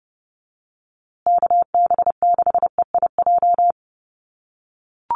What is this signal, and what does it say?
Morse code, heard as a keyed tone, about 28 wpm, 700 Hz. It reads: KB6EIJ
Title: USA Call Signs Repeated 2x 30wpm